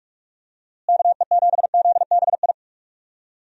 Morse code, heard as a keyed tone, about 48 wpm, 700 Hz. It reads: KE7ZBI
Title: USA Call Signs - 6 Characters - Repeated 2x - 45wpm